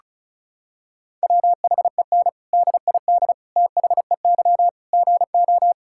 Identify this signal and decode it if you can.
Morse code, heard as a keyed tone, about 36 wpm, 700 Hz. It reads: WHEN DID THEY GO
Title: Sentences from Top 1000 Words - Review Part 5 - 35wpm